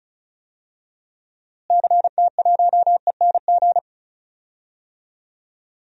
Morse code, heard as a keyed tone, about 36 wpm, 700 Hz. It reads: CT1ENG
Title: DXCC-100 Call Signs - 6 Characters - Repeated 2x - 35wpm